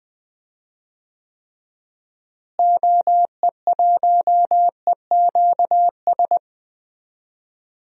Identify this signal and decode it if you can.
Morse code, heard as a keyed tone, about 20 wpm, 700 Hz. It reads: OE1EQS
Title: DXCC-100 Call Signs - 6 Characters - Repeated 3x - 20wpm